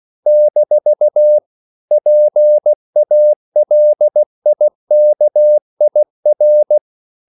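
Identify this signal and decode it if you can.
Morse code, heard as a keyed tone, about 16 wpm, 600 Hz. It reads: - PALIKIR
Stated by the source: CW practice of national capitals at 16 words per minute, from ON6ZQ.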